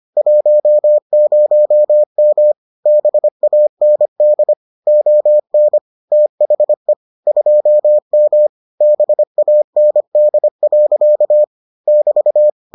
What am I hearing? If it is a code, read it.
Morse code, heard as a keyed tone, about 25 wpm, 600 Hz. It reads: 10M BAND ON THE 2M BAND. <BT>